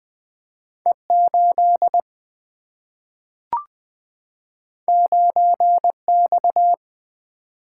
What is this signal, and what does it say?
Morse code, heard as a keyed tone, about 20 wpm, 700 Hz. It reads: E 8 9 X
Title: Single-Letter Single-Number 20wpm